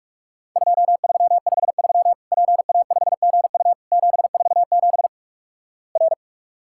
Morse code, heard as a keyed tone, about 48 wpm, 700 Hz. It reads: 2353 PA5GU 747 R
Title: Worked All Europe DX Contest - QTC Exchange - 45wpm